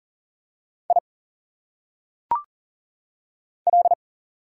Morse code, heard as a keyed tone, about 40 wpm, 700 Hz. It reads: I L
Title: Single Letters - Phonetic Pronunciation 40wpm